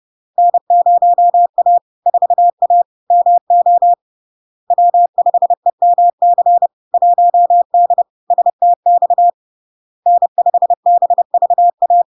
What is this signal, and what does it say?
Morse code, heard as a keyed tone, about 30 wpm, 700 Hz. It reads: N0A 4A MO W5EMC 1D STX N5BVA
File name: LARK-FD-2022-Log-in-CW-30wpm-700-Hz